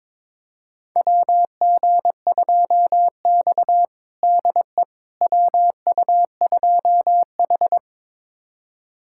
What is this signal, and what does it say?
Morse code, heard as a keyed tone, about 21 wpm, 700 Hz. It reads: WG2X DE WU2H
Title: USA Callsign DE Callsign - 4 Characters - 22wpm